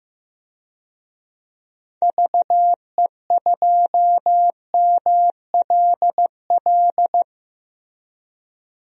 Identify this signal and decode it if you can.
Morse code, heard as a keyed tone, about 15 wpm, 700 Hz. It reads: VE2MLL
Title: DXCC-100 Call Signs - 6 Characters - Repeated 2x - 15wpm